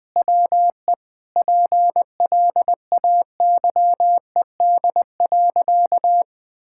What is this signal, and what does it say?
Morse code, heard as a keyed tone, about 20 wpm, 700 Hz. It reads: WE PLAYED.
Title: Sentences with 6-Word Phrases - 20wpm